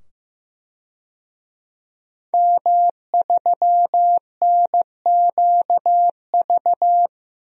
Morse code, heard as a keyed tone, about 15 wpm, 700 Hz. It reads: M3NQV
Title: DXCC-100 Call Signs - 5 Characters - 15wpm